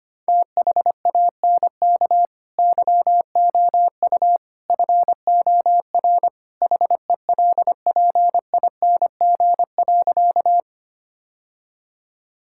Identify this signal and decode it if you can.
Morse code, heard as a keyed tone, about 25 wpm, 700 Hz. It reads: THANK YOU FOR HELPING.
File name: practice / 25wpm / Sentences with 3-Word Phrases - 25wpm